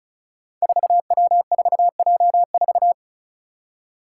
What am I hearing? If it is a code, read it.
Morse code, heard as a keyed tone, about 36 wpm, 700 Hz. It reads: 4W4J4
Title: Five Challenging Letter-Number Combinations 35wpm